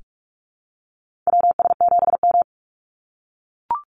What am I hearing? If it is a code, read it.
Morse code, heard as a keyed tone, about 44 wpm, 700 Hz. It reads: WS7M
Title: Call Signs - 45wpm